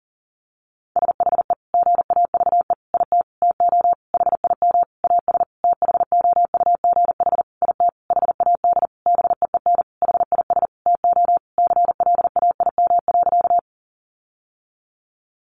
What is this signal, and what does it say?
Morse code, heard as a keyed tone, about 40 wpm, 700 Hz. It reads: SHE GAVE IT TO HIM AS THOUGH IT HAD BEEN HIS TO CLAIM.
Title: Sentences from Top 1500 Words 40wpm